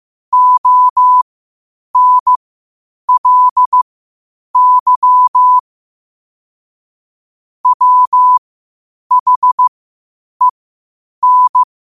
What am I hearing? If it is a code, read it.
Morse code, heard as a keyed tone, about 15 wpm, spaced as for 8 wpm, 1000 Hz. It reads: ONLY WHEN
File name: qotdfiles / 8wpm